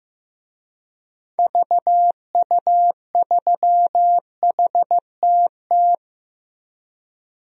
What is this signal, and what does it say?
Morse code, heard as a keyed tone, about 15 wpm, 700 Hz. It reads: VU3HTT
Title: DXCC-100 Call Signs - 6 Characters - Repeated 2x - 15wpm